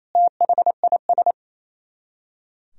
Morse code, heard as a keyed tone, about 35 wpm, 700 Hz. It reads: THIS
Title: Speed-Racing - Sets of 2 Words 28wpm